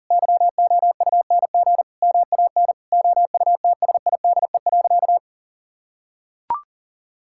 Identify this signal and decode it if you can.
Morse code, heard as a keyed tone, about 40 wpm, 700 Hz. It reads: YOUNG MAN OUTSIDE.
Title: Sentences from Top 1000 Words - Review Part 6 - 40wpm